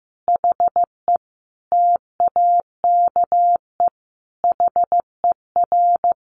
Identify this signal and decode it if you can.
Morse code, heard as a keyed tone, about 15 wpm, 700 Hz. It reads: HE TAKE HER
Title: Sentences from Top 1500 Words 15wpm